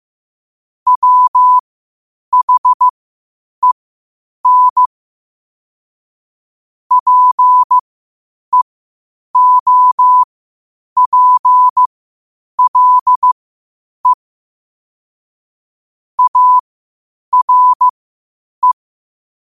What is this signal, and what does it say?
Morse code, heard as a keyed tone, about 15 wpm, spaced as for 8 wpm, 1000 Hz. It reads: WHEN PEOPLE ARE
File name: qotdfiles / 8wpm